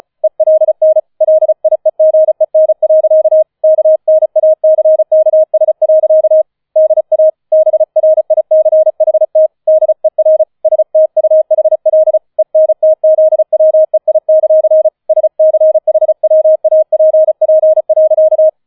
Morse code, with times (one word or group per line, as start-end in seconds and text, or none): 0.2-1.0 ELN
1.2-3.4 LIEGEN.
3.6-6.4 KNACKS.
6.8-7.3 DA
7.5-9.5 BRICHT
9.7-10.4 DER
10.7-12.2 STUHL
12.4-14.9 ENTZWEI;
15.1-18.5 SCHWAPP.